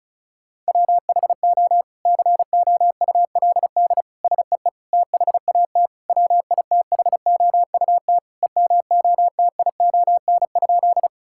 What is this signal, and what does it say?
Morse code, heard as a keyed tone, about 34 wpm, 700 Hz. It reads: WHO COULD SEE THAT WITHOUT EMOTION?
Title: Sentences from Top 1600 Words 35wpm